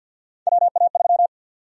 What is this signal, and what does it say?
Morse code, heard as a keyed tone, about 52 wpm, 700 Hz. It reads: WA3
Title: USA Call Sign Prefixes - 50wpm